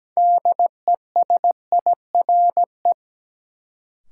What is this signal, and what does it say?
Morse code, heard as a keyed tone, about 17 wpm, 700 Hz. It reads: DESIRE
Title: Sentences from Top 1100 Words - New Word Drill - 17wpm